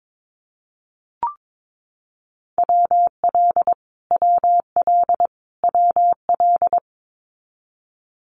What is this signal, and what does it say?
Morse code, heard as a keyed tone, about 22 wpm, 700 Hz. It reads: WL WL WL
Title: Abbreviations Repeated 3x - 22wpm